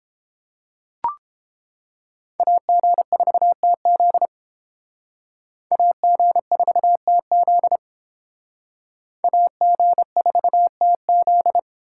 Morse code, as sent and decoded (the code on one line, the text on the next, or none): .- --. ....- - --.. / .- --. ....- - --.. / .- --. ....- - --..
AG4TZ AG4TZ AG4TZ